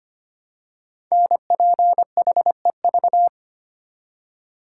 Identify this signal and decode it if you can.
Morse code, heard as a keyed tone, about 25 wpm, 700 Hz. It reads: NPHEV
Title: Code Groups - 5 Letters - 25wpm